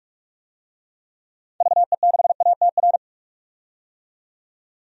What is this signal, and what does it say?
Morse code, heard as a keyed tone, about 46 wpm, 700 Hz. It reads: VE6ATR
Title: DXCC-100 Call Signs - 6 Characters - 45wpm